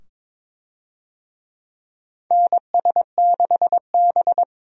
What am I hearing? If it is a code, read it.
Morse code, heard as a keyed tone, about 22 wpm, 700 Hz. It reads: NS6B
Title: USA Call Signs - 4 Characters - 22wpm